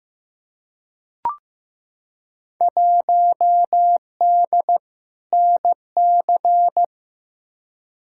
Morse code, heard as a keyed tone, about 15 wpm, 700 Hz. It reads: 1D NC
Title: Field Day Exchanges Encore - 15wpm